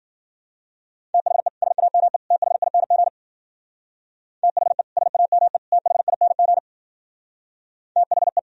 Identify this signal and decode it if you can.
Morse code, heard as a keyed tone, about 48 wpm, 700 Hz. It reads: THE SAME THING THE SAME THING THE
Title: Speed-Racing - Sets of 3 Words 40wpm